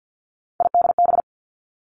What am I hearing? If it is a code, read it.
Morse code, heard as a keyed tone, about 50 wpm, 700 Hz. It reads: IDB
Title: Three Challenging Letter-Number Combinations 50wpm